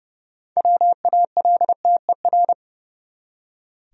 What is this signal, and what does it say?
Morse code, heard as a keyed tone, about 31 wpm, 700 Hz. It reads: WALTER
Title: USA Common Names - 30wpm